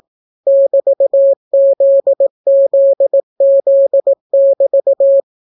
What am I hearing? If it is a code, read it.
Morse code, heard as a keyed tone, about 18 wpm, 550 Hz. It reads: <BT>ZZZ<BT>